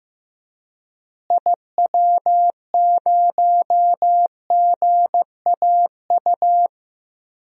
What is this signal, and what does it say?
Morse code, heard as a keyed tone, about 16 wpm, 700 Hz. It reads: IW0GAU
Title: DXCC-100 Call Signs - 6 Characters - Repeated 3x - 15wpm